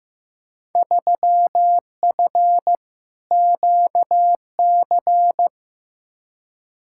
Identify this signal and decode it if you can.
Morse code, heard as a keyed tone, about 16 wpm, 700 Hz. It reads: 3F QC
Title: Field Day Exchanges - 15wpm